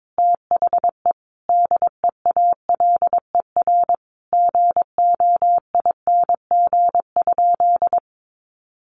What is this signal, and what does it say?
Morse code, heard as a keyed tone, about 22 wpm, 700 Hz. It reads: THE DEALER GOING?
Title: Sentences from Top 1100 Words 22wpm